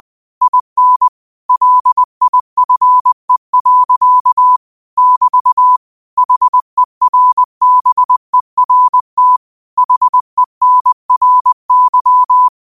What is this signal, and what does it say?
Morse code, heard as a keyed tone, about 20 wpm, 1000 Hz. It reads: IN LIFE. <BT> HERBERT HENRY
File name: qotdfiles / 20wpm